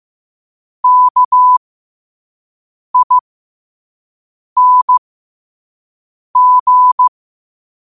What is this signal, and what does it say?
Morse code, heard as a keyed tone, about 15 wpm, spaced as for 5 wpm, 1000 Hz. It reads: KING
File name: qotdfiles / 5wpm